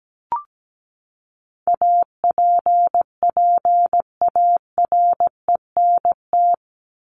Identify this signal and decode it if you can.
Morse code, heard as a keyed tone, about 17 wpm, 700 Hz. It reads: APPARENT
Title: LY Suffix - Top 100 Words with Base Words 17wpm